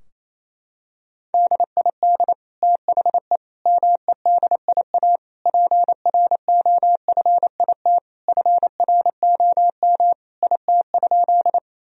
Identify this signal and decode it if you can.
Morse code, heard as a keyed tone, about 27 wpm, 700 Hz. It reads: DID THE MEDIA PROFIT FROM IT?
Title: Sentences from Top 1000 Words - Review Part 4 - 28wpm